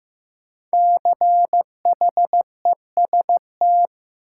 Morse code, heard as a keyed tone, about 16 wpm, 700 Hz. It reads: CHEST
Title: Five Letter Words - 15wpm